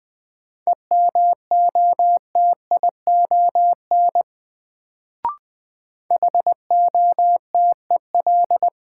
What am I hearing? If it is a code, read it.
Morse code, heard as a keyed tone, about 20 wpm, 700 Hz. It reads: EMOTION HOTEL
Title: Sentences from Top 1600 Words - New Word Drill - 20wpm